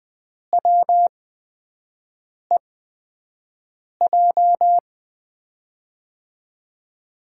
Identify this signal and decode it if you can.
Morse code, heard as a keyed tone, about 20 wpm, spaced as for 5 wpm, 700 Hz. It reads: WEJ
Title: USA Call Signs - 6 Characters - 8x Character Spacing - 20wpm